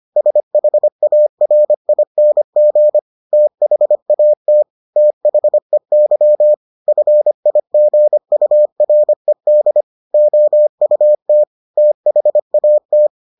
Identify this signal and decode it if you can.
Morse code, heard as a keyed tone, about 25 wpm, 600 Hz. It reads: SHARING THAT THEY FIGURED OUT THAT